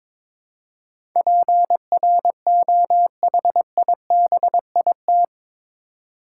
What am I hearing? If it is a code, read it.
Morse code, heard as a keyed tone, about 22 wpm, 700 Hz. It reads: PROHIBIT
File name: practice / 22wpm / Eight Letter Words - 22wpm